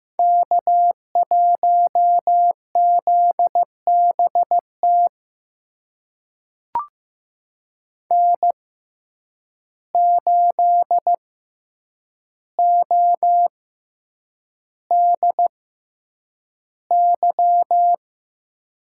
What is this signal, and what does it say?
Morse code, heard as a keyed tone, about 15 wpm, spaced as for 8 wpm, 700 Hz. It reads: K1ZBT N8ODY
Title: USA Call Signs - 5 Characters - 6x Character Spacing - 15wpm